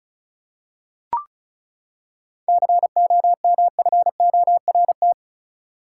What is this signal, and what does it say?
Morse code, heard as a keyed tone, about 36 wpm, 700 Hz. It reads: COMFORT
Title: Seven Letter Words - 35wpm